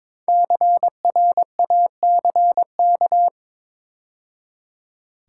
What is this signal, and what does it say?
Morse code, heard as a keyed tone, about 22 wpm, 700 Hz. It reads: CRACK
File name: 5 Letter Words - First 3 Letters Given - 22wpm